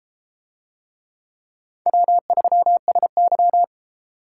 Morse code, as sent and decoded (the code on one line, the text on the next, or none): .-- ...-- ... -.--
W3SY